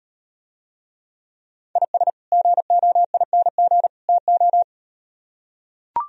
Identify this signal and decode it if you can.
Morse code, heard as a keyed tone, about 38 wpm, 700 Hz. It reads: IS GOING TO